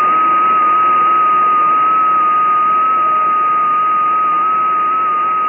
FT8.